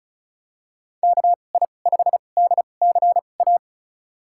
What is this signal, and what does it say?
Morse code, heard as a keyed tone, about 34 wpm, spaced as for 25 wpm, 690 Hz. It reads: KI5DCA
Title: USA Call Signs - 6 Characters - 2x Character Spacing - 35wpm